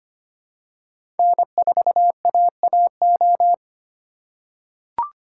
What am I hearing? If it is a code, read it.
Morse code, heard as a keyed tone, about 25 wpm, 700 Hz. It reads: N4AAO
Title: USA Call Signs - 5 Characters - 4x Character Spacing - 25wpm